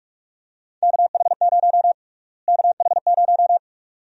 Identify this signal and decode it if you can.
Morse code, heard as a keyed tone, about 46 wpm, 700 Hz. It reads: KH0 KH0
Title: USA Call Sign Prefixes - Repeated 2x - 45wpm